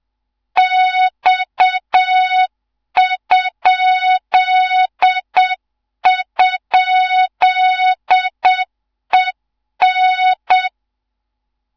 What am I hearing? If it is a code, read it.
Morse code, heard as a keyed tone, about 7 wpm, 3700 Hz, with noise. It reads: X??EN
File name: Morze-HA2MN / hang / 15 GYAKORLAT